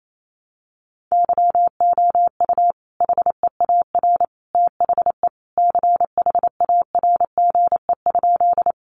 Morse code, heard as a keyed tone, about 28 wpm, 700 Hz. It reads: YOU HEAR THE CHARGE?
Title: Sentences from Top 900 Words 28wpm